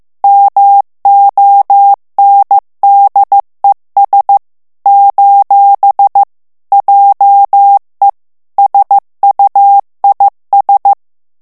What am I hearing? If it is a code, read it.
Morse code, heard as a keyed tone, about 15 wpm, 800 Hz. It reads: MONDES : JE SUIS
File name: FRANCAIS TEXTE CLAIR 2 VITESSE 14 MOTS MINUTES